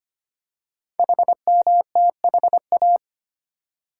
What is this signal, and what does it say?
Morse code, heard as a keyed tone, about 25 wpm, 700 Hz. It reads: HMTHA
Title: Code Groups - 5 Letters - 25wpm